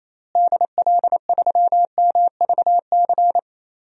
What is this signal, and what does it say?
Morse code, heard as a keyed tone, about 27 wpm, 700 Hz. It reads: DL3MVC
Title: DXCC-100 Call Signs - 6 Characters - Repeated 3x - 28wpm